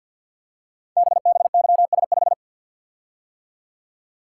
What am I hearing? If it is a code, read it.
Morse code, heard as a keyed tone, about 51 wpm, 700 Hz. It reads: BBYS5
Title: Code Groups - 5 Letter:Numbers - 50wpm